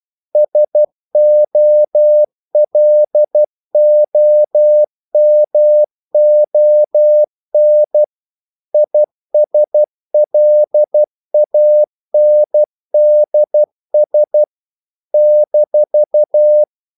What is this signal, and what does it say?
Morse code, heard as a keyed tone, about 12 wpm, 600 Hz. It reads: SOLOMON ISLANDS -
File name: CW / Practice / NationalCapitals12